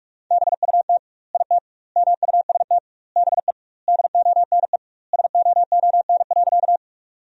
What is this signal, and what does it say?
Morse code, heard as a keyed tone, about 43 wpm, 700 Hz. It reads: BUT IT MUST BE DONE SOON.
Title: 2x Word-Spacing - Sentences from Top 500 Words - 45wpm